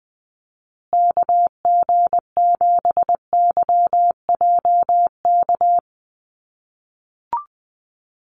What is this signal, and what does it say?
Morse code, heard as a keyed tone, about 20 wpm, 700 Hz. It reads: KG7YJK
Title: USA Call Signs Repeated 3x 20wpm